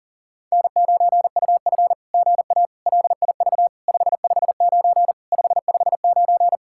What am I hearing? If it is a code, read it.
Morse code, heard as a keyed tone, about 39 wpm, 700 Hz. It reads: N9UF GA LIV 559 559
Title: SOTA QSO Practice 40wpm